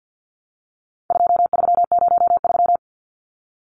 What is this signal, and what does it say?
Morse code, heard as a keyed tone, about 50 wpm, 700 Hz. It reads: 2303
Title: Numbers - 4-Digit 50wpm